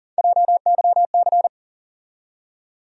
Morse code, heard as a keyed tone, about 40 wpm, 700 Hz. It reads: JYC
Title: USA Call Sign Postfixes - 3 Letters 40wpm